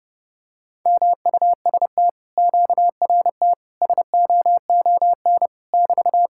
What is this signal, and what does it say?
Morse code, heard as a keyed tone, about 31 wpm, 700 Hz. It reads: MUST QRT SOON <BT>